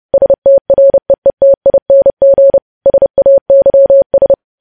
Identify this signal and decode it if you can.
Morse code, heard as a keyed tone, about 30 wpm, 550 Hz. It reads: STREETING SAYS
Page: News Headlines in Morse Code at 30 WPM